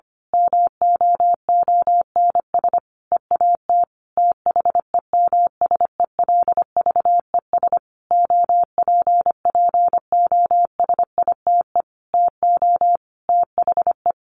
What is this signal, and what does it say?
Morse code, heard as a keyed tone, about 25 wpm, 700 Hz. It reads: MOONS EAT THEMSELVES OPPOSITE TO THE
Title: A Sound Of Thunder 25wpm